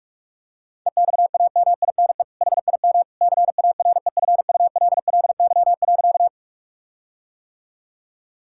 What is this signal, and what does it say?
Morse code, heard as a keyed tone, about 46 wpm, 700 Hz. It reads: EXAMINE HIM CAREFULLY.
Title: Sentences from Top 1600 Words 45wpm